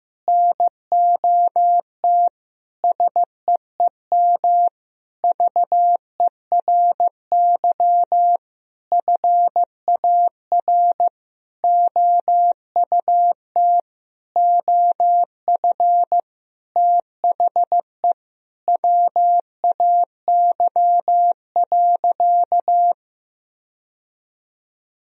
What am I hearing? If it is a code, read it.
Morse code, heard as a keyed tone, about 15 wpm, 700 Hz. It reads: NOT SEEM VERY FAR OUT OF THE WAY.
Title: Sentences from Top 1000 Words - Review Part 1 - 15wpm